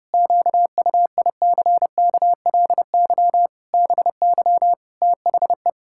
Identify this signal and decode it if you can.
Morse code, heard as a keyed tone, about 30 wpm, 700 Hz. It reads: QUICKLY BY THE
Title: Pangram Challenge v2 - 30wpm